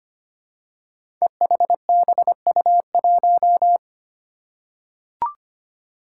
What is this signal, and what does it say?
Morse code, heard as a keyed tone, about 25 wpm, 700 Hz. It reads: EHBU1
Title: Five Challenging Letter-Number Combinations 25wpm